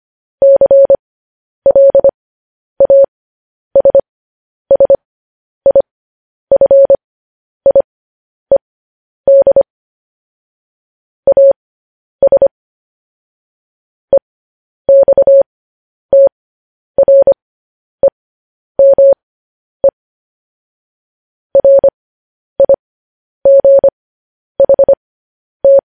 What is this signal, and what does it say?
Morse code, heard as a keyed tone, about 25 wpm, spaced as for 10 wpm, 550 Hz. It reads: CLASSIFIED AS EXTREME RIGHT
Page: news-headlines-in-morse-code-at-10-wpm